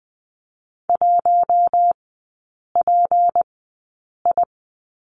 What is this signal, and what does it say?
Morse code, heard as a keyed tone, about 20 wpm, spaced as for 8 wpm, 700 Hz. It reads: 1PI
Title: 152 Lesson 25e - Letter p - Three Characters 20wpm